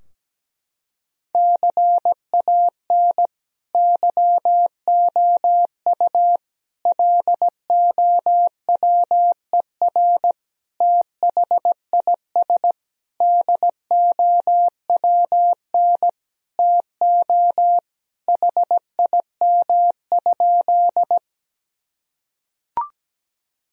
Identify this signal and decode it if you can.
Morse code, heard as a keyed tone, about 17 wpm, 700 Hz. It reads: CAN YOU LOWER THIS DOWN TO HIM?
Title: Sentences from Top 1000 Words - Review Part 4 - 17wpm